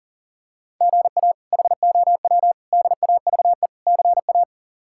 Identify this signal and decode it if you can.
Morse code, heard as a keyed tone, about 40 wpm, 700 Hz. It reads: GA HOW DAVE CA
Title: K1USN SST QSO Practice - 40wpm